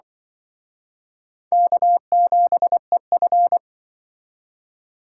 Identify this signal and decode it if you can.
Morse code, heard as a keyed tone, about 24 wpm, 700 Hz. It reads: K7EF
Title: Speed Racing - USA 1x2 Call Signs 20wpm